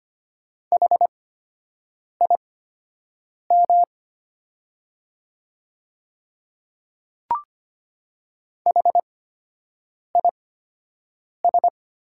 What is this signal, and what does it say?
Morse code, heard as a keyed tone, about 25 wpm, spaced as for 7 wpm, 700 Hz. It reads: HIM HIS
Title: 8x Character Spacing - Top 100 Words - 25wpm